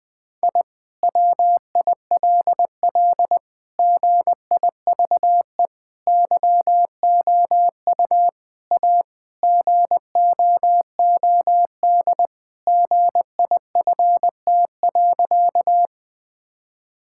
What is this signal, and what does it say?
Morse code, heard as a keyed tone, about 20 wpm, 700 Hz. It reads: I WILL GIVE YOU A GOOD GIFT.
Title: Sentences from Top 1000 Words - Review Part 6 - 20wpm